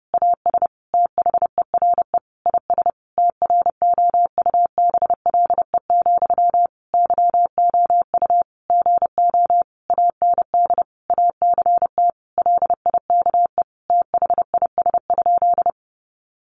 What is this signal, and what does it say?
Morse code, heard as a keyed tone, about 30 wpm, 700 Hz. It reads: AS THERE IS TROUBLE, YOU GO AND ACT LIKE THIS?
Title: Sentences from Top 1200 Words 30wpm